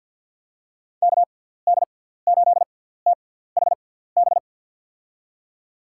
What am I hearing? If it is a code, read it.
Morse code, heard as a keyed tone, about 47 wpm, spaced as for 17 wpm, 700 Hz. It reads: KD8THB
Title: USA Call Signs - 6 Characters - 6x Character Spacing - 50wpm